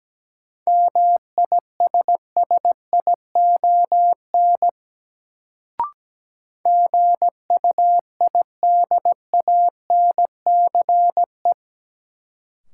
Morse code, heard as a keyed tone, about 17 wpm, 700 Hz. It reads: MISSION GUIDANCE
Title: Sentences from Top 1600 Words - New Word Drill - 17wpm